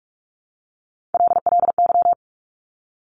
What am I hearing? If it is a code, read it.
Morse code, heard as a keyed tone, about 46 wpm, 700 Hz. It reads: LLY
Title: Top Three Letter Combinations in Words 45wpm